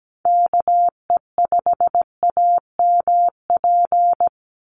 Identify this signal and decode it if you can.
Morse code, heard as a keyed tone, about 17 wpm, 700 Hz. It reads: KE5AMP
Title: USA Call Signs 17wpm